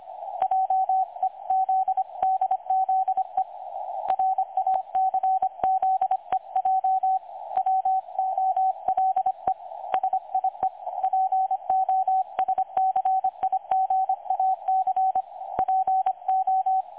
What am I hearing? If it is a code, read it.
Morse code, heard as a keyed tone, about 25 wpm, 760 Hz, with radio noise. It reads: JEZDZE RACZEJ WOLE SIE POSCIGAC PO
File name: dlazaawansowanych / nagraniazpasma / websdr recording 3565.0kHz